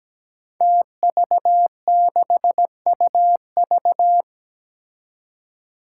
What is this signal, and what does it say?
Morse code, heard as a keyed tone, about 17 wpm, 700 Hz. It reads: TV6UV
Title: Five Challenging Letter-Number Combinations 17wpm